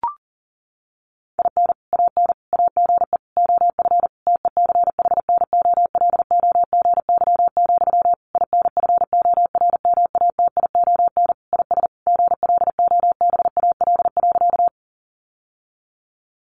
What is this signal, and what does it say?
Morse code, heard as a keyed tone, about 40 wpm, 700 Hz. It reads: IN AN AGE OF TECHNOLOGY, INFORMATION IS GLOBAL.
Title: Sentences from Top 1200 Words 40wpm